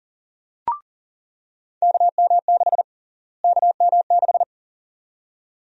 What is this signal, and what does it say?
Morse code, heard as a keyed tone, about 39 wpm, 700 Hz. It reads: KM6 KM6
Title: USA Call Sign Prefixes - Repeated 2x - 40wpm